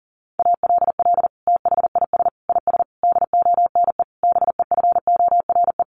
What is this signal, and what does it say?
Morse code, heard as a keyed tone, about 40 wpm, 700 Hz. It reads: ALL THIS IS DONE BEFORE
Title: Sentences from Top 1500 Words 40wpm